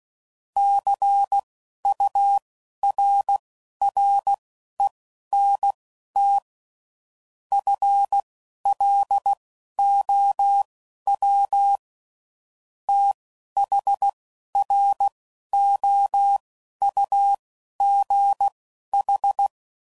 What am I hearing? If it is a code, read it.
Morse code, heard as a keyed tone, about 16 wpm, spaced as for 11 wpm, 800 Hz. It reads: CURRENT FLOW THROUGH
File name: CW Code Groups @10WPM / Technical Subjects / Reactance